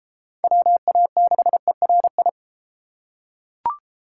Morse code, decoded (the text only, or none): WA6ERI